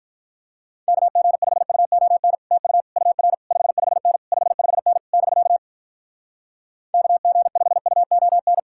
Morse code, heard as a keyed tone, about 48 wpm, 700 Hz. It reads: KG5UON TU UR 55N 55N <BK> KG5UON